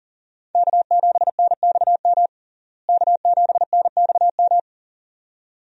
Morse code, heard as a keyed tone, about 40 wpm, 700 Hz. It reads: K7NXM K7NXM
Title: USA Call Signs - 5 Characters - Repeated 2x - 40wpm